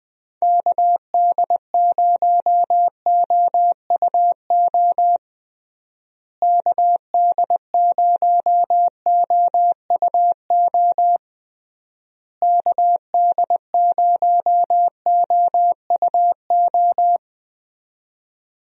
Morse code, heard as a keyed tone, about 21 wpm, 700 Hz. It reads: KD0OUO KD0OUO KD0OUO
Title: USA Call Signs - 6 Characters - Repeated 3x - 20wpm